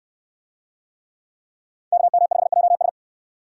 Morse code, heard as a keyed tone, about 64 wpm, 700 Hz. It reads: KM5JS